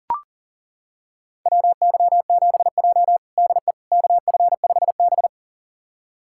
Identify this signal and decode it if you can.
Morse code, heard as a keyed tone, about 40 wpm, 700 Hz. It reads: WY7J DE KF5B
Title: USA Callsign DE Callsign - 4 Characters - 40wpm